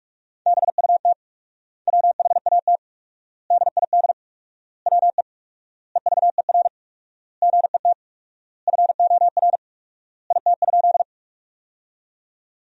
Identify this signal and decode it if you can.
Morse code, heard as a keyed tone, about 47 wpm, 700 Hz. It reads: BUT WHAT DID WE EVER GET FOR IT?
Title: 4x Word-Spacing - Sentences from Top 400 Words - 45wpm